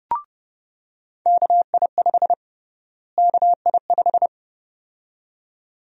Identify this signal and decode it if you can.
Morse code, heard as a keyed tone, about 30 wpm, 700 Hz. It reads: KI5 KI5
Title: USA Call Sign Prefixes - Repeated 2x - 30wpm